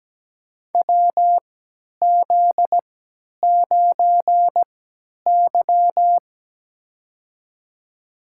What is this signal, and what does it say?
Morse code, heard as a keyed tone, about 17 wpm, spaced as for 10 wpm, 700 Hz. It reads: WZ9Y
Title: USA Call Signs - 4 Characters - 3x Character Spacing - 17wpm